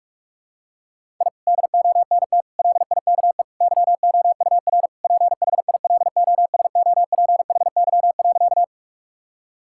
Morse code, heard as a keyed tone, about 41 wpm, 700 Hz. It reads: I DONT LIKE YOUR PHILOSOPHY.
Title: Sentences from Top 1000 Words - Review Part 6 - 45wpm